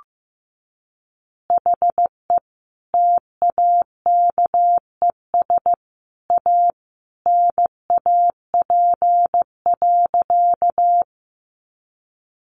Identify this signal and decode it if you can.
Morse code, heard as a keyed tone, about 15 wpm, 700 Hz. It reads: HE TAKES A NAP.
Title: Easy Phrases 15wpm